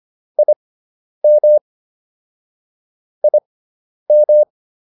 Morse code, heard as a keyed tone, about 25 wpm, spaced as for 10 wpm, 600 Hz. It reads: IM IM